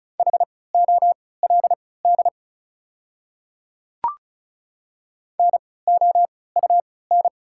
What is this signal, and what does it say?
Morse code, heard as a keyed tone, about 34 wpm, spaced as for 20 wpm, 690 Hz. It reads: HOLD NOUN
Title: 3x Character Spacing - Top 500 Words - 35wpm